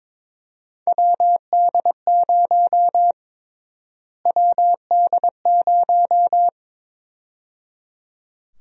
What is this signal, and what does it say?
Morse code, heard as a keyed tone, about 22 wpm, 700 Hz. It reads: WD0 WD0
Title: USA Call Sign Prefixes - Repeated 2x - 22wpm